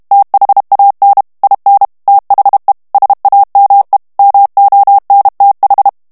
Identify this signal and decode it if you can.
Morse code, heard as a keyed tone, about 32 wpm, 800 Hz. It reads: THAN IN THE SAME MONTH
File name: ANGLAIS texte clair 2 vitesse 30 mots minute